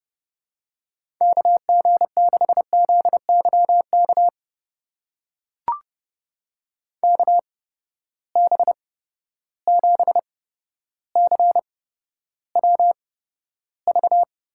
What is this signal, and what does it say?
Morse code, heard as a keyed tone, about 31 wpm, spaced as for 13 wpm, 700 Hz. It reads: KG6ZYK KB7CWV
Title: USA Call Signs - 6 Characters - 8x Character Spacing - 30wpm